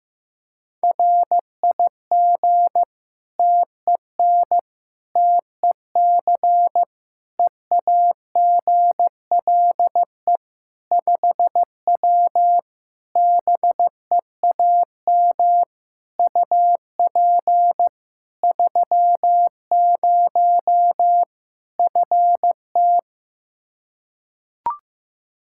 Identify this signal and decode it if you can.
Morse code, heard as a keyed tone, about 15 wpm, 700 Hz. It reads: RIG TEN TEC EAGLE 5W BEAM UP 30 FT
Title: Practice QSO - Exchange 2 - Info - 15wpm